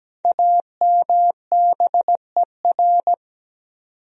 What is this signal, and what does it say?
Morse code, heard as a keyed tone, about 17 wpm, 700 Hz. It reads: AMBER
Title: USA Common Names - 17wpm